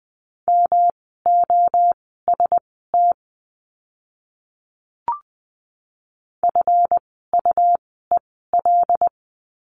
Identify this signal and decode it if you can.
Morse code, heard as a keyed tone, about 20 wpm, spaced as for 14 wpm, 700 Hz. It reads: MOST FUEL
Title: 139 Lesson 23f - Letter f - Words 20wpm